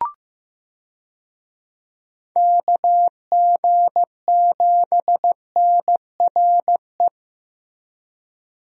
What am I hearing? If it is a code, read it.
Morse code, heard as a keyed tone, about 15 wpm, 700 Hz. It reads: KG7NRE
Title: USA Call Signs - 6 Characters - Repeated 3x - 15wpm